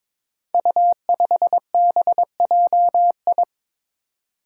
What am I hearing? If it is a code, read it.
Morse code, heard as a keyed tone, about 22 wpm, 700 Hz. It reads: U5BJI
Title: Five Challenging Letter-Number Combinations 22wpm